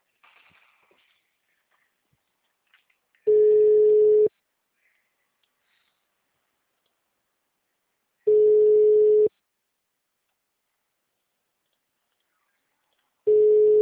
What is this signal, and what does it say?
Ringback tone (Europe, 425 Hz).